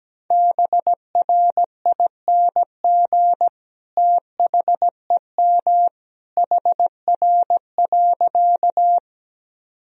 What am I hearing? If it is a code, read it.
Morse code, heard as a keyed tone, about 17 wpm, 700 Hz. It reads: BRING THEM HR.
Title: Sentences from Top 500 Words and Abbreviations 17wpm